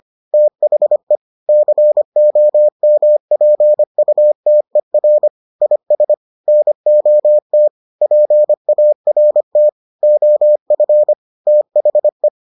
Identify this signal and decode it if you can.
Morse code, heard as a keyed tone, about 25 wpm, 600 Hz. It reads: THE COMPUTER IS NOT PART OF THE